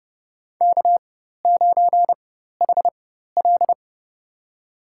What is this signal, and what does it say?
Morse code, heard as a keyed tone, about 30 wpm, spaced as for 14 wpm, 700 Hz. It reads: K9HL
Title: USA Call Signs - 4 Characters - 4x Character Spacing - 30wpm